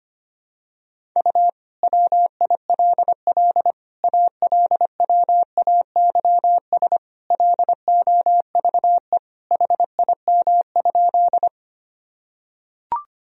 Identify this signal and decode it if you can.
Morse code, heard as a keyed tone, about 25 wpm, 700 Hz. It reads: U WILL ALWAYS LOVE HIM?
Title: Sentences from Top 400 Words and Abbreviations 25wpm